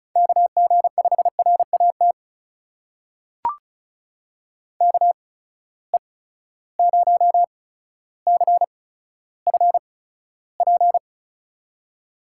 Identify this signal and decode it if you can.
Morse code, heard as a keyed tone, about 36 wpm, spaced as for 15 wpm, 700 Hz. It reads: KG5RAT KE0CFP
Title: USA Call Signs - 6 Characters - 8x Character Spacing - 35wpm